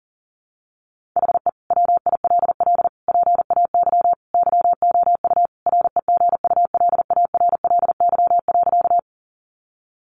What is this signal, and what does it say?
Morse code, heard as a keyed tone, about 37 wpm, 700 Hz. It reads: HE WILL PAY YOU REGULARLY.
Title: Sentences from Top 1400 Words 40wpm